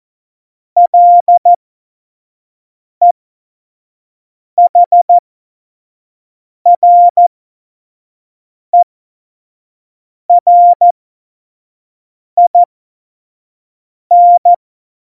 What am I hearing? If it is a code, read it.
Morse code, heard as a keyed tone, about 14 wpm, spaced as for 5 wpm, 700 Hz. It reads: LEHRERIN
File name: Morsekurs / Witze / Witz 04. Juni 2019, 25 F70